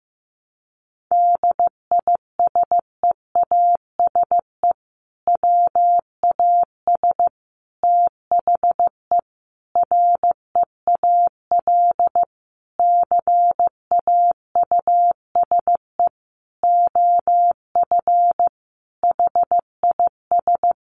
Morse code, heard as a keyed tone, about 15 wpm, 700 Hz. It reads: DISEASE WAS THE REAL CAUSE OF HIS
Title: Sentences from Top 1100 Words 15wpm